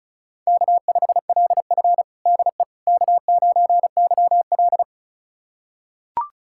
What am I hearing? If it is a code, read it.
Morse code, heard as a keyed tone, about 36 wpm, 700 Hz. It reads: K5LF DE K9YL
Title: USA Callsign DE Callsign - 4 Characters - 35wpm